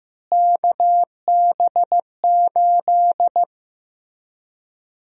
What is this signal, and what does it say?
Morse code, heard as a keyed tone, about 15 wpm, 700 Hz. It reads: KB8
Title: USA Call Sign Prefixes - Repeated 2x - 15wpm